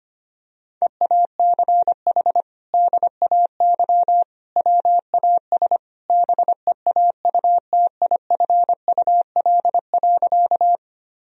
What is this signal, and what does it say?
Morse code, heard as a keyed tone, about 25 wpm, 700 Hz. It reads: EACH DAY WAS BEAUTIFUL.
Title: Sentences from Top 1000 Words - Review Part 1 - 25wpm